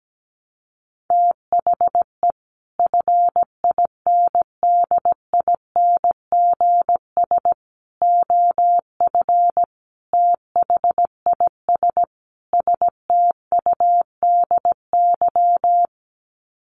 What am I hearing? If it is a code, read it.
Morse code, heard as a keyed tone, about 17 wpm, 700 Hz. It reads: THE FINDINGS OF THIS STUDY
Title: Sets of 5 Words Encore2 17wpm